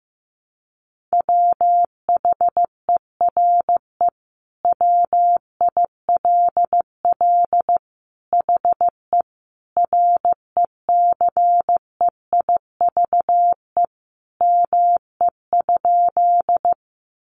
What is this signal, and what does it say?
Morse code, heard as a keyed tone, about 15 wpm, 700 Hz. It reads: WHERE WILL HE RECEIVE ME?
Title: Sentences from Top 1000 Words 15wpm